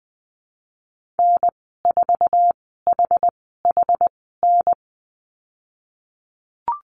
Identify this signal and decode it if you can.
Morse code, heard as a keyed tone, about 20 wpm, spaced as for 15 wpm, 700 Hz. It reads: N4HHN
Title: 082 Lesson 14g - Number 2 - Call Signs 20wpm